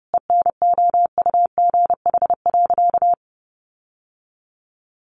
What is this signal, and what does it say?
Morse code, heard as a keyed tone, about 28 wpm, 700 Hz. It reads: ENOUGH.
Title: Sentences from Top 1000 Words 30wpm